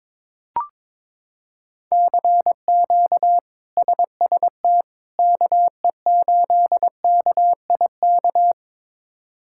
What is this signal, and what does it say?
Morse code, heard as a keyed tone, about 22 wpm, 700 Hz. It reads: CQ SST KE8KIK
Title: K1USN SST QSO Practice - 22wpm